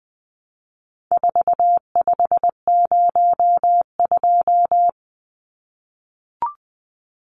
Morse code, heard as a keyed tone, about 20 wpm, 700 Hz. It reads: 4502
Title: Numbers - 4-Digit 20wpm